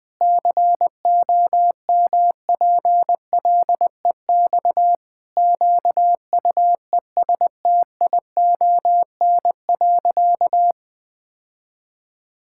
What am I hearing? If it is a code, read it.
Morse code, heard as a keyed tone, about 20 wpm, 700 Hz. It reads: COMPLEX QUESTION.
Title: Sentences from Top 1000 Words - Review Part 6 - 20wpm